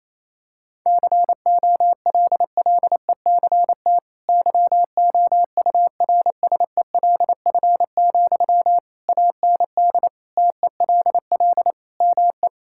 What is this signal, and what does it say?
Morse code, heard as a keyed tone, about 28 wpm, 700 Hz. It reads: COLLECT YOURSELF, AND TELL ME
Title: Sentences from Top 1600 Words 28wpm